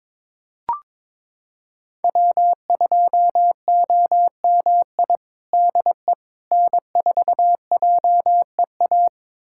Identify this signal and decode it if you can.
Morse code, heard as a keyed tone, about 21 wpm, 700 Hz. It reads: W2OMI DE N4JEA
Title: USA Callsign DE Callsign - 5 Characters - 22wpm